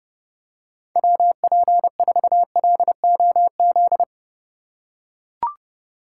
Morse code, heard as a keyed tone, about 30 wpm, 700 Hz. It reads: WP4LOZ
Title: USA Call Signs - 6 Characters - 2x Character Spacing - 30wpm